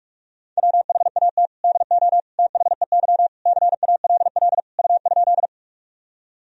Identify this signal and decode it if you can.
Morse code, heard as a keyed tone, about 46 wpm, 700 Hz. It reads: WHAT DO THEY CALL U?
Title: Sentences from Top 100 Words and Abbreviations 45wpm